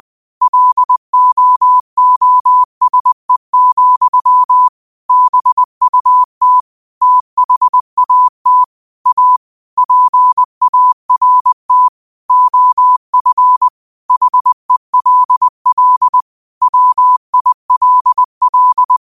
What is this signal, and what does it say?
Morse code, heard as a keyed tone, about 20 wpm, 1000 Hz. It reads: LOOSE, BUT THAT A PART OF HELL WILL
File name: qotdfiles / 20wpm